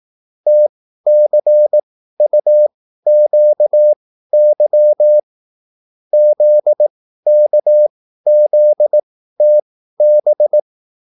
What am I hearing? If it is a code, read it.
Morse code, heard as a keyed tone, about 18 wpm, spaced as for 13 wpm, 600 Hz. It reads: TCUQY ZKZTB
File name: QSL / CW lesson